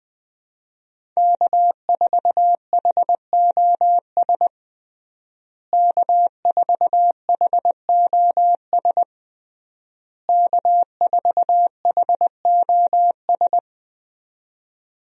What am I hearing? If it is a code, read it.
Morse code, heard as a keyed tone, about 21 wpm, 700 Hz. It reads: K4HOS K4HOS K4HOS
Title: USA Call Signs - 5 Characters - Repeated 3x - 20wpm